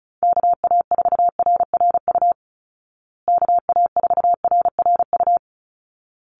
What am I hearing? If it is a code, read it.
Morse code, heard as a keyed tone, about 34 wpm, 700 Hz. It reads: KA4RRU KA4RRU
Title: Call Signs Repeated 3x - 35wpm